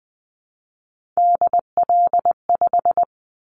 Morse code, heard as a keyed tone, about 20 wpm, 700 Hz. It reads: DL5
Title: Callsign Prefixes for Top 50 DXCC - 20wpm